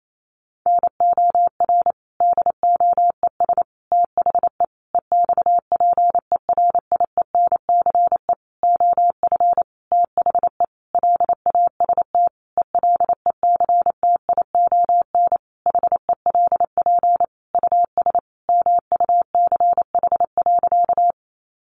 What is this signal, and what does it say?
Morse code, heard as a keyed tone, about 28 wpm, 700 Hz. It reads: NOR DOES THE EXPERIENCE OF THE LAST ELECTION HELP US MUCH.
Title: Sentences from Top 1500 Words 28wpm